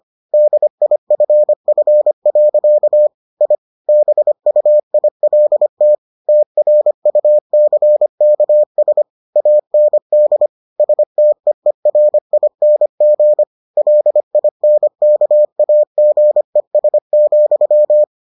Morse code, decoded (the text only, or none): DIFF. I BUILT TRUCKS AND STEERING LINKAGES,